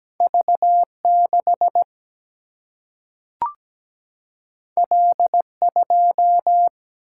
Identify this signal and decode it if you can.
Morse code, heard as a keyed tone, about 17 wpm, 700 Hz. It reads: V6 L2
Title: Single-Letter Single-Number 17wpm